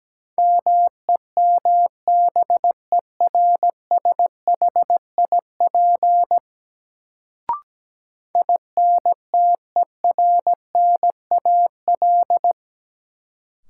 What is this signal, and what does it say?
Morse code, heard as a keyed tone, about 17 wpm, 700 Hz. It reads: MEMBERSHIP INTERNAL
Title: Sentences from Top 1400 Words - New Word Drill Alt - 17wpm